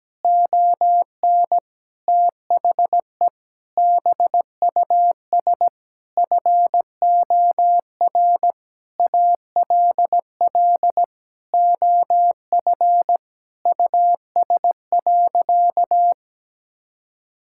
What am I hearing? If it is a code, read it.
Morse code, heard as a keyed tone, about 17 wpm, 700 Hz. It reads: ON THE BUS FOR ALL OF US.
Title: Sentences from Top 1000 Words - Review Part 6 - 17wpm